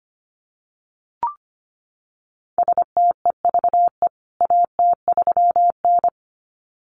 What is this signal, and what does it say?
Morse code, heard as a keyed tone, about 25 wpm, 700 Hz. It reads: STEVE AT3N
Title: CWT Exchange 25wpm